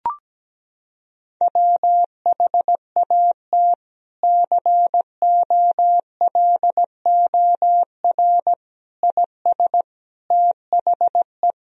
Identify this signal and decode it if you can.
Morse code, heard as a keyed tone, about 17 wpm, 700 Hz. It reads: WHAT COLOR IS THE
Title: Sentences from Top 1000 Words - Review Part 3 - 17wpm